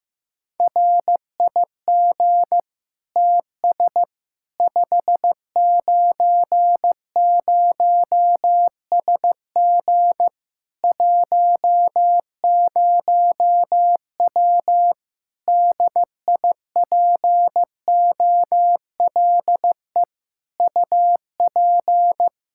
Morse code, heard as a keyed tone, about 15 wpm, 700 Hz. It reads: RIG TS 590SG 10W DIPOLE UP
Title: K1USN SST QSO Practice 15wpm